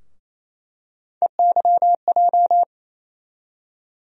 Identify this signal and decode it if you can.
Morse code, heard as a keyed tone, about 27 wpm, 700 Hz. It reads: EYJ
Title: USA Call Sign Postfixes - 3 Letters 28wpm